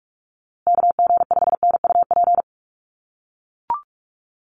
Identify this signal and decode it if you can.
Morse code, heard as a keyed tone, about 44 wpm, 700 Hz. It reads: KG5NUP
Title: USA Call Signs Repeated 2x 45wpm